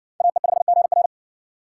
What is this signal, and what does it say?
Morse code, heard as a keyed tone, about 58 wpm, 700 Hz. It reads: AE5GR